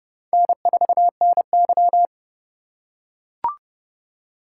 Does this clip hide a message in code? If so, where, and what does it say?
Yes: Morse code from 0.3 to 2.1 s, heard as a keyed tone, about 29 wpm, 700 Hz. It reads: N4NY